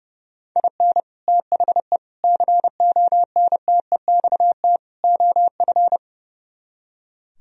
Morse code, heard as a keyed tone, about 38 wpm, 700 Hz. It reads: IN THE CONTEXT OF